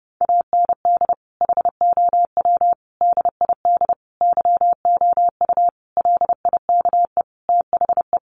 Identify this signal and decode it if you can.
Morse code, heard as a keyed tone, about 30 wpm, 700 Hz. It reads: AND HOW DID YOU LIKE THE
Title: Sentences from Top 1200 Words 30wpm